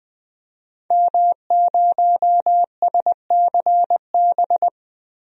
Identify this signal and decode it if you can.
Morse code, heard as a keyed tone, about 20 wpm, 700 Hz. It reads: M0SCB
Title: DXCC-100 Call Signs - 5 Characters - 20wpm